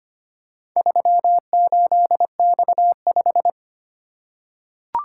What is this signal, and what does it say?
Morse code, heard as a keyed tone, about 25 wpm, 700 Hz. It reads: 38X5
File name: practice / 25wpm / Four Letter-Number Combinations - 25wpm